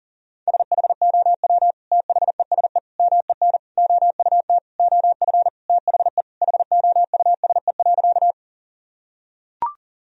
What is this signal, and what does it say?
Morse code, heard as a keyed tone, about 40 wpm, 700 Hz. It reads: SHOW THESE MEN OUT OF THE HOUSE.
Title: Sentences from Top 1000 Words - Review Part 1 - 40wpm